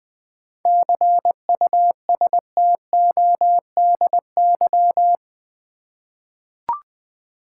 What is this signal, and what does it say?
Morse code, heard as a keyed tone, about 20 wpm, 700 Hz. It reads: CUSTODY
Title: 7 Letter Words - First 3 Letters Given - 20wpm